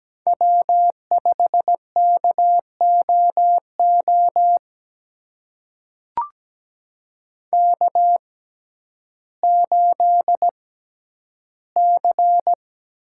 Morse code, heard as a keyed tone, about 17 wpm, spaced as for 10 wpm, 700 Hz. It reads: W5KOO K8C
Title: USA Call Signs - 6x Character Spacing - 17wpm